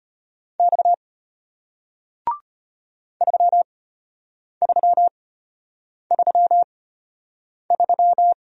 Morse code, heard as a keyed tone, about 34 wpm, spaced as for 8 wpm, 700 Hz. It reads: X 3333